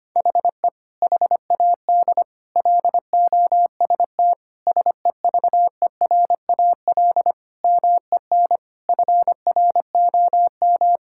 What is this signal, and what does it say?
Morse code, heard as a keyed tone, about 25 wpm, 700 Hz. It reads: HE HAD LOST SEVERAL MEN FROM
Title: Sentences from Top 1600 Words 25wpm